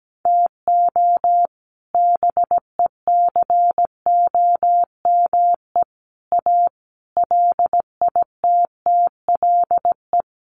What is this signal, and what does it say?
Morse code, heard as a keyed tone, about 17 wpm, 700 Hz. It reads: TO BECOME A LITTLE
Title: Sentences from Top 1300 Words 17wpm